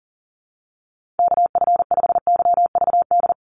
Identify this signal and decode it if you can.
Morse code, heard as a keyed tone, about 41 wpm, 700 Hz. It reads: KF5YVD
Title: USA Call Signs 40wpm